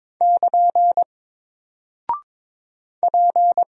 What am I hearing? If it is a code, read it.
Morse code, heard as a keyed tone, about 22 wpm, 700 Hz. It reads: <KN> P
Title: Single Character (Letter:Number:Punctuation:Prosign) 22wpm